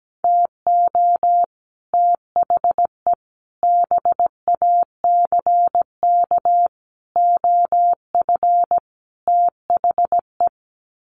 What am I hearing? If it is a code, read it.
Morse code, heard as a keyed tone, about 17 wpm, 700 Hz. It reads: TO THE BACK OF THE
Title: Sets of 5 Words 17wpm